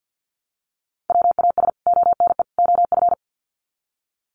Morse code, heard as a keyed tone, about 50 wpm, 700 Hz. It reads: WAS ONE OF
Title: Sets of 3 Words 50wpm